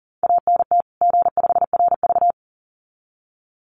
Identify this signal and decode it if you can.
Morse code, heard as a keyed tone, about 40 wpm, 700 Hz. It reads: ANT G5RV
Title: Common QSO Phrases 40wpm